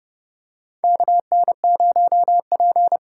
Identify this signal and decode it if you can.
Morse code, heard as a keyed tone, about 30 wpm, 700 Hz. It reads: KN0P
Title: USA Call Signs - 4 Characters - 4x Character Spacing - 30wpm